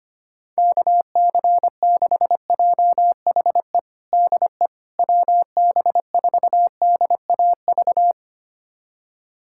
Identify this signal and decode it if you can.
Morse code, heard as a keyed tone, about 25 wpm, 700 Hz. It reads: KC6JHE DE WB4DAV
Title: USA Callsign DE Callsign 25wpm